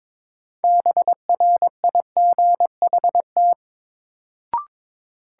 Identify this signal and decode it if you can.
Morse code, heard as a keyed tone, about 22 wpm, 700 Hz. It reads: BRIGHT
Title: Sentences from Top 1000 Words - New Word Drill - 22wpm